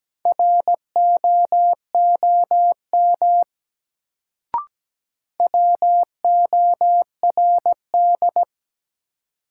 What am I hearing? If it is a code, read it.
Morse code, heard as a keyed tone, about 17 wpm, 700 Hz. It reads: ROOM WORD
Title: Sentences from Top 300 Words - New Word Drill - 17wpm